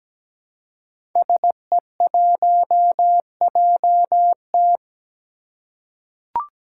Morse code, heard as a keyed tone, about 17 wpm, 700 Hz. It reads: SE1JT
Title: Five Challenging Letter-Number Combinations 17wpm